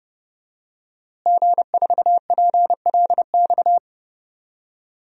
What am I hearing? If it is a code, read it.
Morse code, heard as a keyed tone, about 30 wpm, 700 Hz. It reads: G4PLX
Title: DXCC-100 Call Signs - 5 Characters - Repeated 3x - 30wpm